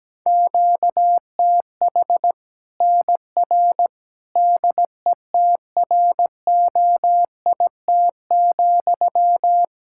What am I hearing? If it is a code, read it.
Morse code, heard as a keyed tone, about 17 wpm, 700 Hz. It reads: QTH NR DETROIT,
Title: K1USN SST QSO Practice 17wpm